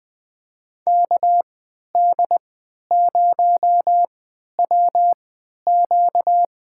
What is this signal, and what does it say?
Morse code, heard as a keyed tone, about 20 wpm, spaced as for 11 wpm, 700 Hz. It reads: KD0WQ
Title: USA Call Signs - 3x Character Spacing - 20wpm